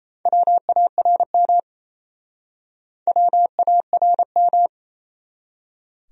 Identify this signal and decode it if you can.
Morse code, heard as a keyed tone, about 38 wpm, 700 Hz. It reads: WARM WARM